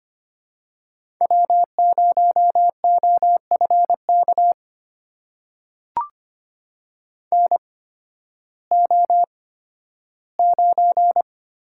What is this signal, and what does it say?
Morse code, heard as a keyed tone, about 25 wpm, spaced as for 13 wpm, 700 Hz. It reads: W0OFK NO9
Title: USA Call Signs - 8x Character Spacing - 25wpm